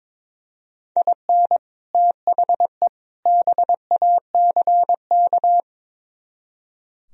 Morse code, heard as a keyed tone, about 24 wpm, 700 Hz. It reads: IN THE BACK
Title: Speed-Racing - Sets of 3 Words 22wpm